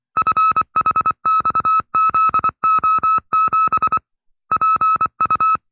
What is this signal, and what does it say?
Morse code, heard as a keyed tone, about 24 wpm, 1310 Hz. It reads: FHXZO7 PU
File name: FreeFiles / CW-practice / 20WPM-side2